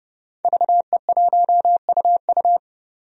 Morse code, heard as a keyed tone, about 30 wpm, 700 Hz. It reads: VE1UU
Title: DXCC-100 Call Signs - 5 Characters - 30wpm